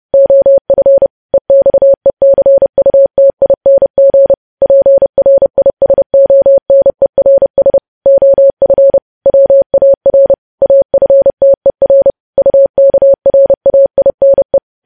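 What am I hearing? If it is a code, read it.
Morse code, heard as a keyed tone, about 30 wpm, 550 Hz. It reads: OF EXECUTING PRISONERS OF WAR AFTER UKRAINE
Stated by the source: Morse code news headlines practice sent at 30 wpm.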